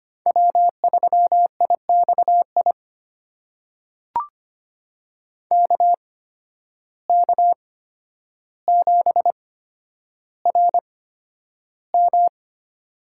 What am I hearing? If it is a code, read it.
Morse code, heard as a keyed tone, about 25 wpm, spaced as for 11 wpm, 700 Hz. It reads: W3IXI KK7RM
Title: USA Call Signs - 5 Characters - 8x Character Spacing - 25wpm